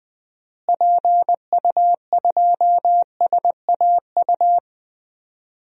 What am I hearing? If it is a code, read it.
Morse code, heard as a keyed tone, about 20 wpm, 700 Hz. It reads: PU2SAU
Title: DXCC-100 Call Signs - 6 Characters - Repeated 2x - 20wpm